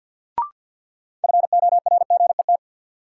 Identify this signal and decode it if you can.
Morse code, heard as a keyed tone, about 52 wpm, 700 Hz. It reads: FORGET
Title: Sentences from Top 1200 Words - New Word Drill - 50wpm